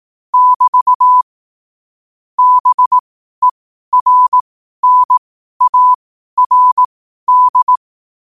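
Morse code, heard as a keyed tone, about 18 wpm, spaced as for 12 wpm, 1000 Hz. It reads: <BT> BERNARD
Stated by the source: Morse code at 12 WPM.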